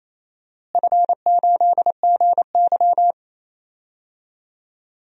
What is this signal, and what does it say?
Morse code, heard as a keyed tone, about 28 wpm, 700 Hz. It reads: F8GY
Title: DXCC-100 Call Signs - 4 Characters - Repeated 2x - 28wpm